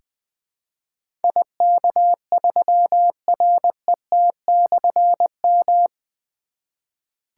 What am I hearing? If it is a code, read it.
Morse code, heard as a keyed tone, about 20 wpm, 700 Hz. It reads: IK3RET/M
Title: Difficult Call Signs 20wpm